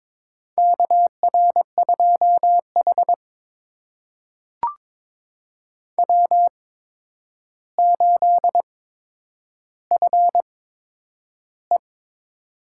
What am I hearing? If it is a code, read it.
Morse code, heard as a keyed tone, about 22 wpm, spaced as for 9 wpm, 700 Hz. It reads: KR2H W8FE
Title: USA Call Signs - 4 Characters - 8x Character Spacing - 22wpm